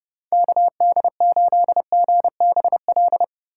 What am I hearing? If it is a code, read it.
Morse code, heard as a keyed tone, about 32 wpm, 700 Hz. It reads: KD8GBL